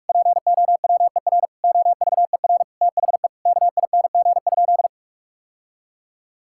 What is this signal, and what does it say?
Morse code, heard as a keyed tone, about 42 wpm, 700 Hz. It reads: POWER OVER THE KING?
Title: Sentences from Top 1000 Words - Review Part 6 - 45wpm